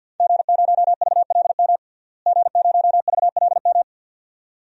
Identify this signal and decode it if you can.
Morse code, heard as a keyed tone, about 50 wpm, 700 Hz. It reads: G0VLM G0VLM
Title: DXCC-100 Call Signs - 5 Characters - Repeated 2x - 50wpm